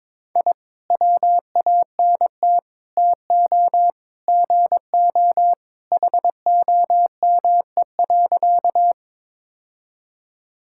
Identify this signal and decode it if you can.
Morse code, heard as a keyed tone, about 22 wpm, 700 Hz. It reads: I WANT TO GO HOME.
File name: practice / 22wpm / Sentences with 3-Word Phrases - 22wpm